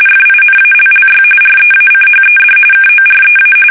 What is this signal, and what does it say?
RTTY, about 45.45 baud. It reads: FINALLY REACHED KOREA
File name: fsk850 / battleshipnj